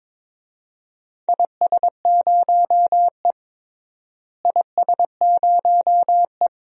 Morse code, heard as a keyed tone, about 22 wpm, 700 Hz. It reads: IS0E IS0E
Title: DXCC-100 Call Signs - 4 Characters - Repeated 2x - 22wpm